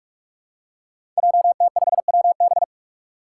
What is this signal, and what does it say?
Morse code, heard as a keyed tone, about 46 wpm, 700 Hz. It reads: JT5WB
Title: Five Challenging Letter-Number Combinations 45wpm